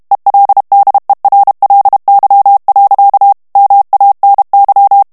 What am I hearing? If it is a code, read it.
Morse code, heard as a keyed tone, about 32 wpm, 800 Hz. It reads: ELDERLY. MANY
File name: ANGLAIS texte clair 2 vitesse 30 mots minute